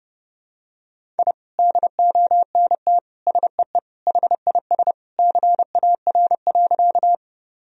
Morse code, heard as a keyed tone, about 30 wpm, 700 Hz. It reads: I DONT SEE HIS CAR.
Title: Sentences from Top 1000 Words - Review Part 1 - 30wpm